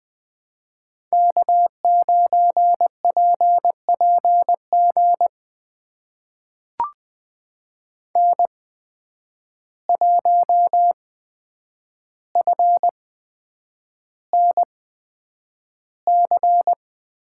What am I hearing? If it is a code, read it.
Morse code, heard as a keyed tone, about 20 wpm, spaced as for 9 wpm, 700 Hz. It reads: K9PPG N1FNC
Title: USA Call Signs - 5 Characters - 8x Character Spacing - 20wpm